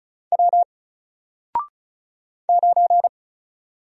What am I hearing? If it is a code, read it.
Morse code, heard as a keyed tone, about 36 wpm, 700 Hz. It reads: W 9
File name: Single-Letter-Number-35wpm